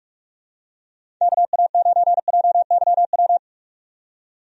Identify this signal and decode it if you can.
Morse code, heard as a keyed tone, about 46 wpm, 700 Hz. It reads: KA9JYW
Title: USA Call Signs - 6 Characters - 45wpm